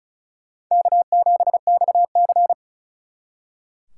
Morse code, heard as a keyed tone, about 45 wpm, 700 Hz. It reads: K7XC